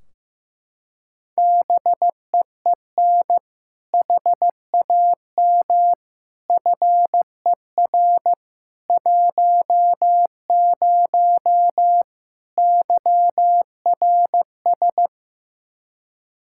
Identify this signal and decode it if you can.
Morse code, heard as a keyed tone, about 16 wpm, 700 Hz. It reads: BEEN HAM FER 10 YRS
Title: Practice QSO - Exchange 3 - More Info - 15wpm